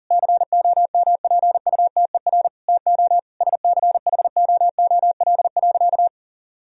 Morse code, heard as a keyed tone, about 40 wpm, 700 Hz. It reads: COMPUTER TO SCHOOL.
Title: Sentences from Top 1000 Words - Review Part 3 - 40wpm